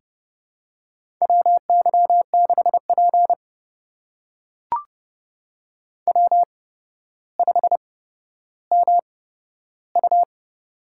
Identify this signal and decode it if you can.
Morse code, heard as a keyed tone, about 30 wpm, spaced as for 13 wpm, 700 Hz. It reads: WY6P W5MU
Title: USA Call Signs - 8x Character Spacing - 30wpm